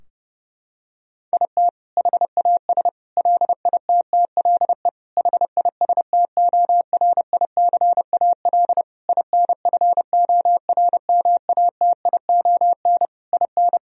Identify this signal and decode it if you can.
Morse code, heard as a keyed tone, about 30 wpm, 700 Hz. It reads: IT HAS LITTLE HISTORICAL INFORMATION IN
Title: Sentences from Top 1000 Words - Review Part 4 - 30wpm